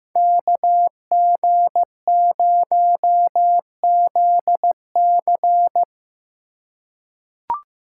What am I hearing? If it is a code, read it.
Morse code, heard as a keyed tone, about 15 wpm, 700 Hz. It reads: KG0ZC
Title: USA Call Signs - 5 Characters - 3x Character Spacing - 15wpm